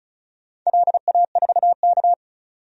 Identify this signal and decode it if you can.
Morse code, heard as a keyed tone, about 36 wpm, 700 Hz. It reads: LA4K
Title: DXCC-100 Call Signs - 4 Characters - 35wpm